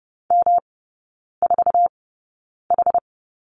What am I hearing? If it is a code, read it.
Morse code, heard as a keyed tone, about 30 wpm, spaced as for 9 wpm, 700 Hz. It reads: M4H
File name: Lesson 18e - Letter m - Three Characters 30wpm